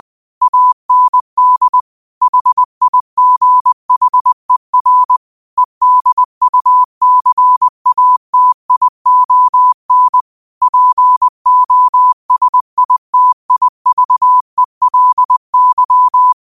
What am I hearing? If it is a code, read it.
Morse code, heard as a keyed tone, about 20 wpm, 1000 Hz. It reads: AND HIGHER EDUCATION POSITIVELY
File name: qotdfiles / 20wpm